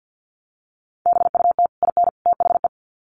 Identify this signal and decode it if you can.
Morse code, heard as a keyed tone, about 50 wpm, 700 Hz. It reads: BUT IN THE
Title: Sets of 3 Words 50wpm